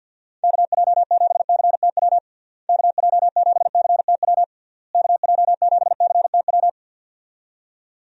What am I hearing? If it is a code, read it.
Morse code, heard as a keyed tone, about 51 wpm, 700 Hz. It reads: KJ7CTW KJ7CTW KJ7CTW
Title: USA Call Signs - 6 Characters - Repeated 3x - 50wpm